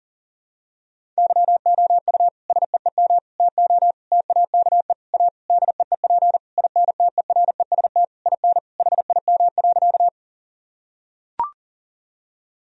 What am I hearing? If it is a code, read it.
Morse code, heard as a keyed tone, about 40 wpm, 700 Hz. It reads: YOU SEEM TO TAKE A DEEP INTEREST IN HIM.
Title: Sentences from Top 1000 Words - Review Part 3 - 40wpm